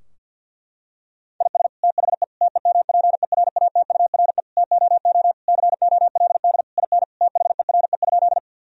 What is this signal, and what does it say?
Morse code, heard as a keyed tone, about 50 wpm, 700 Hz. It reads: IS THE TEMPERATURE TOO COLD IN THERE?
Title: Sentences from Top 1000 Words - Review Part 4 - 50wpm